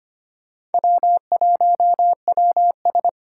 Morse code, heard as a keyed tone, about 31 wpm, 700 Hz. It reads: W1WS
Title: Speed Racing - USA 1x2 Call Signs 25wpm